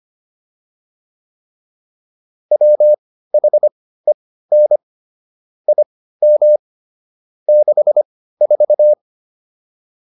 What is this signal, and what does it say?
Morse code, heard as a keyed tone, about 25 wpm, spaced as for 15 wpm, 600 Hz. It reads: WHEN IM 64
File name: PR305 15